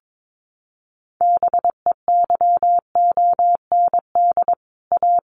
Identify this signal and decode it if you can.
Morse code, heard as a keyed tone, about 22 wpm, 700 Hz. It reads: BEYOND A
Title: Sentences from Top 1500 Words 22wpm